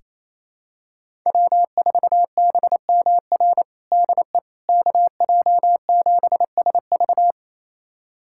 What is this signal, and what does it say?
Morse code, heard as a keyed tone, about 28 wpm, 700 Hz. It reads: W4BMR DE KJ7SV
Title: USA Callsign DE Callsign - 5 Characters - 28wpm